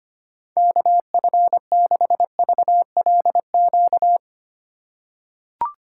Morse code, heard as a keyed tone, about 25 wpm, 700 Hz. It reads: KF6VLQ
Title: USA Call Signs - 6 Characters - 4x Character Spacing - 25wpm